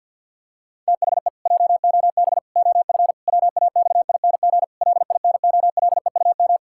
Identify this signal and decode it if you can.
Morse code, heard as a keyed tone, about 50 wpm, 700 Hz. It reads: THE JOB OF WAXING LINOLEUM
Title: Pangram Challenge v2 - 50wpm